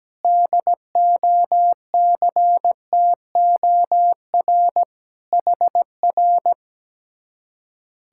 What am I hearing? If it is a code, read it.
Morse code, heard as a keyed tone, about 17 wpm, 700 Hz. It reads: DOCTOR HR
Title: Speed-Racing Common QSO Phrases - 17wpm